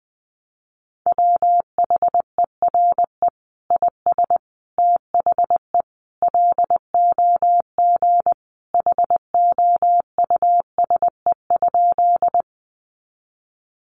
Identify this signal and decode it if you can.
Morse code, heard as a keyed tone, about 20 wpm, 700 Hz. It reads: WHERE IS THE LOG HOUSE?
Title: Sentences from Top 1400 Words 20wpm